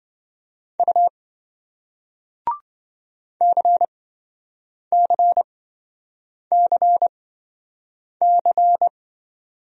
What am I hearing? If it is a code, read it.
Morse code, heard as a keyed tone, about 27 wpm, spaced as for 7 wpm, 700 Hz. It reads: U CCCC